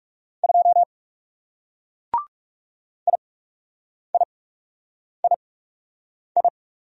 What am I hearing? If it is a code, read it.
Morse code, heard as a keyed tone, about 39 wpm, spaced as for 8 wpm, 700 Hz. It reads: 2 IIII